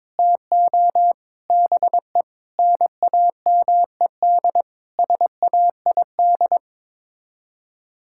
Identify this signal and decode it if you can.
Morse code, heard as a keyed tone, about 22 wpm, 700 Hz. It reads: TO BE NAMED SAID
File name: practice / 22wpm / Sets of 8 Words - 22wpm